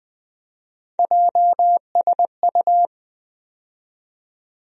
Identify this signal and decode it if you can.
Morse code, heard as a keyed tone, about 20 wpm, 700 Hz. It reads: JSU
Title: USA Call Sign Postfixes - 3 Letters 20wpm